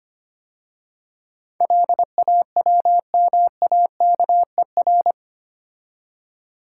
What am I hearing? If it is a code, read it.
Morse code, heard as a keyed tone, about 25 wpm, 700 Hz. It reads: LAWMAKER
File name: 8 Letter Words - First 3 Letters Given - 25wpm